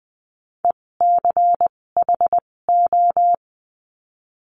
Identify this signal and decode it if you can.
Morse code, heard as a keyed tone, about 20 wpm, spaced as for 16 wpm, 700 Hz. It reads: ECHO
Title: 256 Lesson 42a - Words 12-20wpm